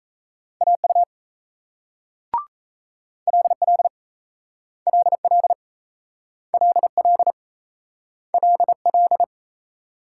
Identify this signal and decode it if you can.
Morse code, heard as a keyed tone, about 37 wpm, 700 Hz. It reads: AU LL LL LL LL